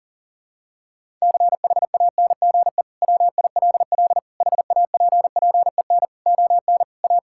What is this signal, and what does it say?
Morse code, heard as a keyed tone, about 40 wpm, 700 Hz. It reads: CHANGE WILL HAPPEN ON A
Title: Sentences from Top 1000 Words - Review Part 5 - 40wpm